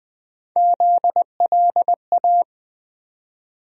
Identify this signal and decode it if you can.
Morse code, heard as a keyed tone, about 20 wpm, 700 Hz. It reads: ZLA
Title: USA Call Sign Postfixes - 3 Letters 20wpm